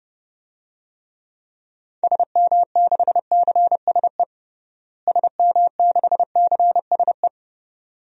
Morse code, heard as a keyed tone, about 31 wpm, 700 Hz. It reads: SM6CSE SM6CSE
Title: DXCC-100 Call Signs - 6 Characters - Repeated 2x - 30wpm